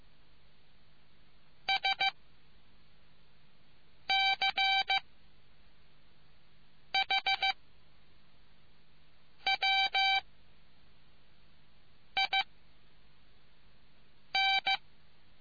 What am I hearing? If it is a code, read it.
Morse code, heard as a keyed tone, about 15 wpm, 780 Hz. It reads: S C H W I N